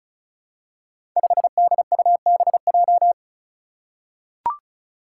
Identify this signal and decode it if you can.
Morse code, heard as a keyed tone, about 36 wpm, 700 Hz. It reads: 5DUBJ
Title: Five Challenging Letter-Number Combinations 35wpm